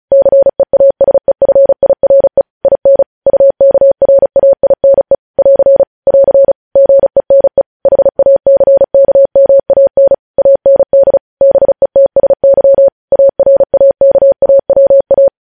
Morse code, heard as a keyed tone, about 36 wpm, 550 Hz. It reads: CEASEFIRE IN UKRAINE <AR> <AR> GENE HACKMAN AND BETSY ARAKAWA
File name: podcast / News headlines-1x-35x35